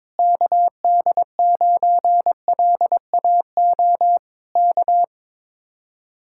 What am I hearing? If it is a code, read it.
Morse code, heard as a keyed tone, about 21 wpm, 700 Hz. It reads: KD9LAO K
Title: Speed-Racing Common QSO Phrases - 22wpm